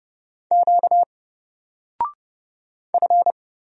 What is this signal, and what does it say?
Morse code, heard as a keyed tone, about 30 wpm, 700 Hz. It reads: Q F
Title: Single Character (Letter:Number:Punctuation:Prosign) 30wpm